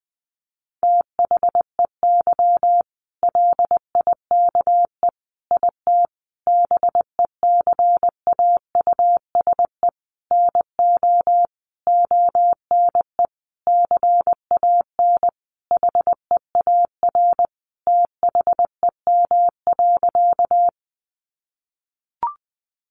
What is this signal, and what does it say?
Morse code, heard as a keyed tone, about 20 wpm, 700 Hz. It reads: THEY LIKE IT BECAUSE NO ONE CAN HEAR THEM.
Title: Sentences from Top 600 Words 20wpm